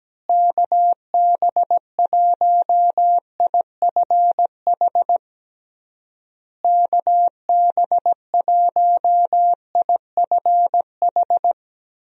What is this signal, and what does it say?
Morse code, heard as a keyed tone, about 17 wpm, 700 Hz. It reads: KB1IFH KB1IFH
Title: USA Call Signs - 6 Characters - Repeated 3x - 17wpm